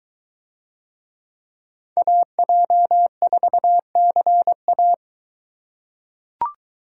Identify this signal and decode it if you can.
Morse code, heard as a keyed tone, about 22 wpm, 700 Hz. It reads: AJ4CA